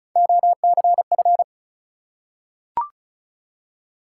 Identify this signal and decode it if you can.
Morse code, heard as a keyed tone, about 36 wpm, 700 Hz. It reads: OCF
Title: Top 180 QSO Elements 35wpm